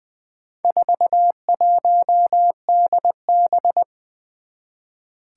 Morse code, heard as a keyed tone, about 20 wpm, 700 Hz. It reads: 41DB
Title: Four Challenging Letter-Number Combinations 20wpm